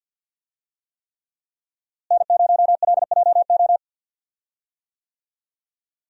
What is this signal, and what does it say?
Morse code, heard as a keyed tone, about 50 wpm, 700 Hz. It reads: N0LJO
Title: USA Call Signs - 5 Characters - Repeated 3x - 50wpm